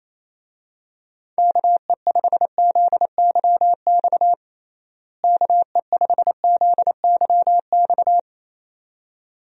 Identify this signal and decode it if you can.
Morse code, heard as a keyed tone, about 28 wpm, 700 Hz. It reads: KE5ZYX KE5ZYX
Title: USA Call Signs - 6 Characters - Repeated 2x - 28wpm